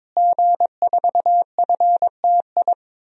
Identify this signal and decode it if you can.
Morse code, heard as a keyed tone, about 22 wpm, 700 Hz. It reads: G4FTI
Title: DXCC-100 Call Signs - 5 Characters - Repeated 3x - 22wpm